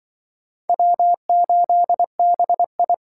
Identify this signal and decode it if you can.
Morse code, heard as a keyed tone, about 24 wpm, 700 Hz. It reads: W8BI
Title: Speed Racing - USA 1x2 Call Signs 20wpm